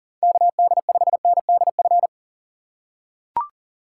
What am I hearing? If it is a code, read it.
Morse code, heard as a keyed tone, about 40 wpm, 700 Hz. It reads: KD5NDF
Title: USA Call Signs - 6 Characters - 3x Character Spacing - 40wpm